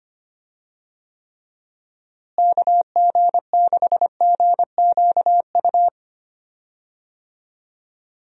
Morse code, heard as a keyed tone, about 25 wpm, 700 Hz. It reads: KG6GQU
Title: USA Call Signs - 6 Characters - Repeated 3x - 25wpm